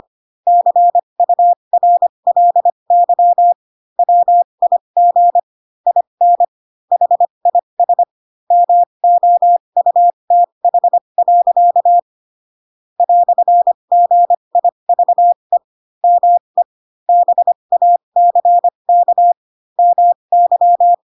Morse code, heard as a keyed tone, about 25 wpm, spaced as for 22 wpm, 700 Hz. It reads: CURLY WIG IN HIS MOUTH. "GIVE ME BACK MY
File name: Pinocchio / Pinocchio ch2 18 25 ik1hge